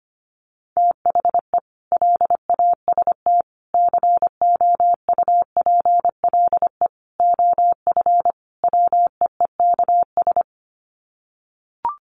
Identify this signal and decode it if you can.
Morse code, heard as a keyed tone, about 25 wpm, 700 Hz. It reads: THE LAST COUPLE OF WEEKS
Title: Sets of 5 Words Encore2 25wpm